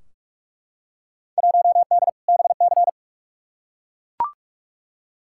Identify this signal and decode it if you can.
Morse code, heard as a keyed tone, about 42 wpm, 700 Hz. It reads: 1D BC
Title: Field Day Exchanges - 45wpm